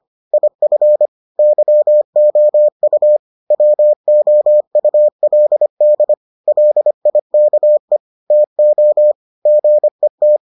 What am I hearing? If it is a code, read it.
Morse code, heard as a keyed tone, about 25 wpm, 600 Hz. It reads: IF YOU WOULD LIKE TO GET